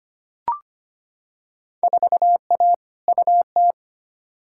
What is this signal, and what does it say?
Morse code, heard as a keyed tone, about 25 wpm, 700 Hz. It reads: 4A UT
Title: Field Day Exchanges - 25wpm